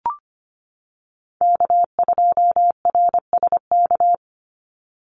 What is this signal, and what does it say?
Morse code, heard as a keyed tone, about 25 wpm, 700 Hz. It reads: K2RSK
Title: Call Signs - 25wpm